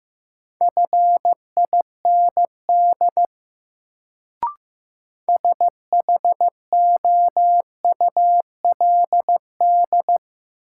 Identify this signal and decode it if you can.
Morse code, heard as a keyed tone, about 15 wpm, 700 Hz. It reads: FIND SHOULD
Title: Sets of 2 Words - New Word Drill - 15wpm